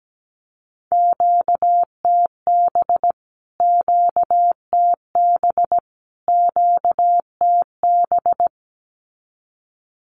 Morse code, heard as a keyed tone, about 17 wpm, 700 Hz. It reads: QTB QTB QTB
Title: Q-Codes Repeated 3x 17wpm